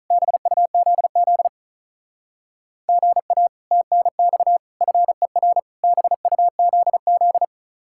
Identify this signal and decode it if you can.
Morse code, heard as a keyed tone, about 45 wpm, 700 Hz. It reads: BUZZ GA TNX FER BUZZ